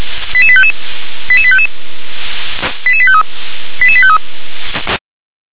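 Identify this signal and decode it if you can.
Five-tone selective call (ZVEI), heard as quick runs of five tones, tones of about 70 ms. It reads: 88855 88855 88853 88853
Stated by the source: ZVEI five-tone selective call, code 88641, received with an SDR.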